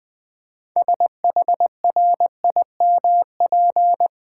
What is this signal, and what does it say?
Morse code, heard as a keyed tone, about 20 wpm, 700 Hz. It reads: SHRIMP
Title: Easy Words with Hints - 20wpm